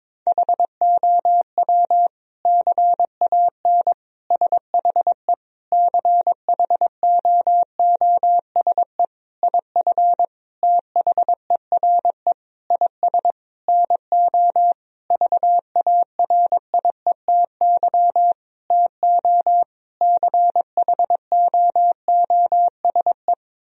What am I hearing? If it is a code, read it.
Morse code, heard as a keyed tone, about 22 wpm, 700 Hz. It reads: HOW CAN SHE CHOOSE IF THERE IS NO VARIETY TO CHOOSE
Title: Sentences from Top 1000 Words - Review Part 4 - 22wpm